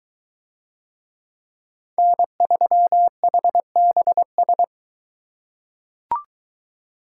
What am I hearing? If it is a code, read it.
Morse code, heard as a keyed tone, about 20 wpm, 700 Hz. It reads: N3HBS